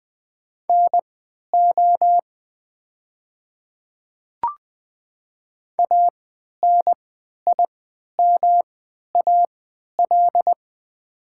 Morse code, heard as a keyed tone, about 20 wpm, spaced as for 11 wpm, 700 Hz. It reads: NO ANIMAL
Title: 3x Character Spacing - Top 200 Words - 20wpm